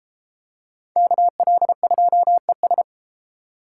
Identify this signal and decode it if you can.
Morse code, heard as a keyed tone, about 30 wpm, 700 Hz. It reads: KL2ES